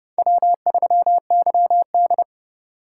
Morse code, heard as a keyed tone, about 37 wpm, 690 Hz. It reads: W3YD
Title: Speed Racing - USA Call Signs 30wpm